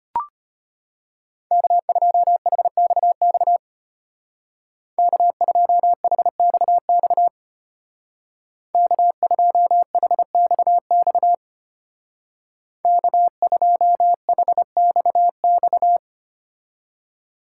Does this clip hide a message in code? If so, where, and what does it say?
Yes: Morse code from 1.5 to 16.0 s, heard as a keyed tone, about 32 wpm, 700 Hz. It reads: K2HXX K2HXX K2HXX K2HXX